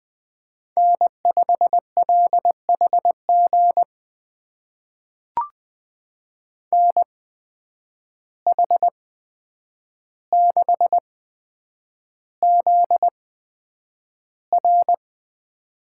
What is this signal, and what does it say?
Morse code, heard as a keyed tone, about 20 wpm, spaced as for 9 wpm, 700 Hz. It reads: N5LHG NH6ZR
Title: USA Call Signs - 5 Characters - 8x Character Spacing - 20wpm